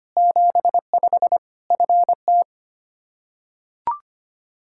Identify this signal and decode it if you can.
Morse code, heard as a keyed tone, about 25 wpm, 700 Hz. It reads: 75 FT
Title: Practice QSO - Exchange 2 - Info - 25wpm